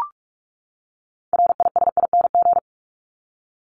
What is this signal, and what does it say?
Morse code, heard as a keyed tone, about 44 wpm, 700 Hz. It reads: RISING
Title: ING Suffix - Top 100 Words with Base Words 45wpm